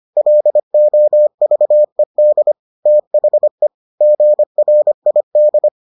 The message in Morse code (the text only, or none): LOVED THE GRID